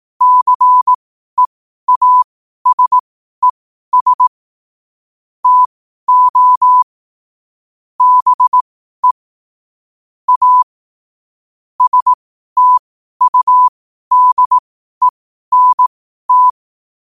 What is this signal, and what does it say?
Morse code, heard as a keyed tone, about 18 wpm, spaced as for 12 wpm, 1000 Hz. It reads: CEASES TO BE A STUDENT
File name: qotdfiles / 12wpm